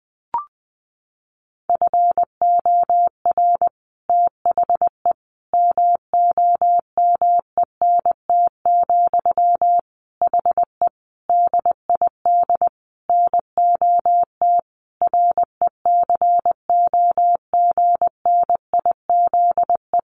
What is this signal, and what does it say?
Morse code, heard as a keyed tone, about 20 wpm, 700 Hz. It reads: FOR THE MOMENT, HE DID NOT RECOGNIZE
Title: Sentences from Top 1400 Words 20wpm